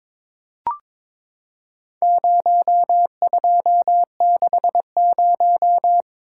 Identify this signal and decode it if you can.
Morse code, heard as a keyed tone, about 23 wpm, 700 Hz. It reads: 0260
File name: practice / 22wpm / Four Letter-Number Combinations - 22wpm